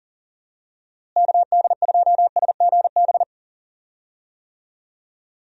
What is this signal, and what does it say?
Morse code, heard as a keyed tone, about 40 wpm, 700 Hz. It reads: KD2SGB
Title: USA Call Signs - 6 Characters - Repeated 2x - 40wpm